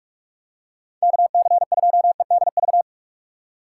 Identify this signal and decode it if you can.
Morse code, heard as a keyed tone, about 41 wpm, 700 Hz. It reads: KC2EDV
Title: USA Call Signs - 2x Character Spacing - 45wpm